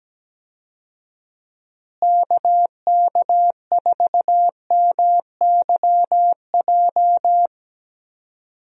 Morse code, heard as a keyed tone, about 17 wpm, 700 Hz. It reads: KK4MYJ
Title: USA Call Signs - 6 Characters - Repeated 2x - 17wpm